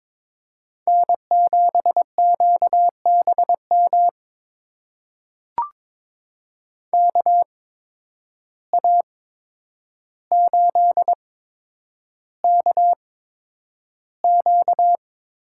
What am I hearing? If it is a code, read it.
Morse code, heard as a keyed tone, about 22 wpm, spaced as for 9 wpm, 700 Hz. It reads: N7QBM KA8KQ
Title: USA Call Signs - 8x Character Spacing - 22wpm